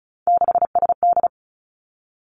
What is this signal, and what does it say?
Morse code, heard as a keyed tone, about 36 wpm, 700 Hz. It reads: 6SD
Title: Three Challenging Letter-Number Combinations 35wpm